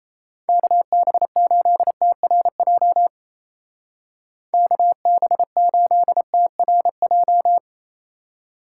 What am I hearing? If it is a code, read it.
Morse code, heard as a keyed tone, about 35 wpm, 700 Hz. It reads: KB8TRJ KB8TRJ